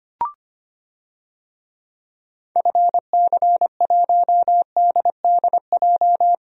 Morse code, heard as a keyed tone, about 25 wpm, 700 Hz. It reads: FC1DDJ
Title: DXCC-100 Call Signs - 6 Characters - 25wpm